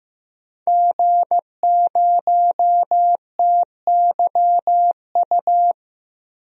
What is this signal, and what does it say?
Morse code, heard as a keyed tone, about 15 wpm, 700 Hz. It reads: G0TYU
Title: DXCC-100 Call Signs - 5 Characters - Repeated 2x - 15wpm